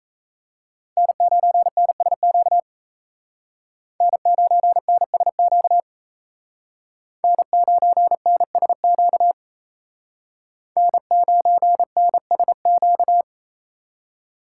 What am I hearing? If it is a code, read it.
Morse code, heard as a keyed tone, about 35 wpm, 700 Hz. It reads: N9NSQ N9NSQ N9NSQ N9NSQ
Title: Speed Racing - USA 1x3 Call Signs 28wpm